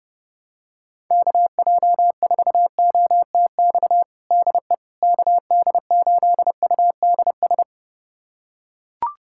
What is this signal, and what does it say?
Morse code, heard as a keyed tone, about 29 wpm, 700 Hz. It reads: KJ4OTX DE KD8UDS
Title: USA Callsign DE Callsign - 6 Characters - 30wpm